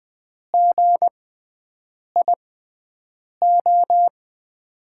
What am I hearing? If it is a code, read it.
Morse code, heard as a keyed tone, about 20 wpm, spaced as for 7 wpm, 700 Hz. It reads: GIO
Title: USA Call Signs - 6x Character Spacing - 20wpm